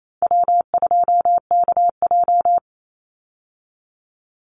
Morse code, heard as a keyed tone, about 28 wpm, 700 Hz. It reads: W2KJ
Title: Easy Call Signs Repeated 3x - 28wpm